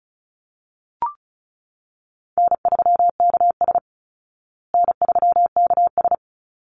Morse code, heard as a keyed tone, about 34 wpm, 700 Hz. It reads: N3KS N3KS
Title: Easy Call Signs Repeated 3x - 35wpm